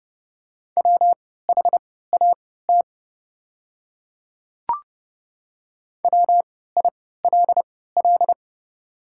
Morse code, heard as a keyed tone, about 30 wpm, spaced as for 17 wpm, 700 Hz. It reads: WHAT WILL
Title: 3x Character Spacing - Top 100 Words - 30wpm